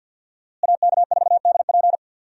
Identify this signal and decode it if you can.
Morse code, heard as a keyed tone, about 50 wpm, 700 Hz. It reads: AK4DP
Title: USA Call Signs - 5 Characters - Repeated 3x - 50wpm